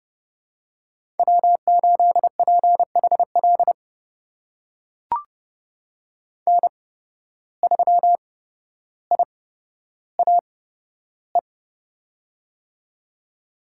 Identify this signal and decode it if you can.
Morse code, heard as a keyed tone, about 30 wpm, spaced as for 13 wpm, 700 Hz. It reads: W8PHL N3IAE
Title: USA Call Signs - 5 Characters - 8x Character Spacing - 30wpm